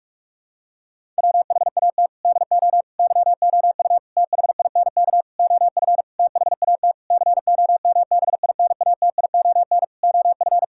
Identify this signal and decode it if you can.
Morse code, heard as a keyed tone, about 41 wpm, 700 Hz. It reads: WHAT DO YOU THINK OF THAT COMBINATION OF
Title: Sentences from Top 1000 Words - Review Part 6 - 45wpm